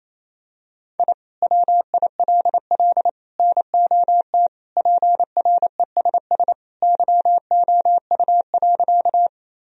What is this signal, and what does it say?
Morse code, heard as a keyed tone, about 28 wpm, 700 Hz. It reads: I WILL NOT PRESS YOU.
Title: Sentences from Top 1000 Words - Review Part 1 - 28wpm